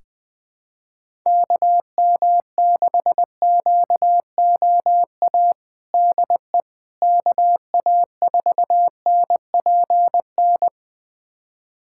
Morse code, heard as a keyed tone, about 20 wpm, 700 Hz. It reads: KM6QOA DE KA4NPN
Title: USA Callsign DE Callsign - 6 Characters - 20wpm